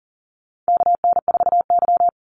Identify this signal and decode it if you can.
Morse code, heard as a keyed tone, about 40 wpm, 700 Hz. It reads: KN4Y
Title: Call Signs Repeated 2x - 40wpm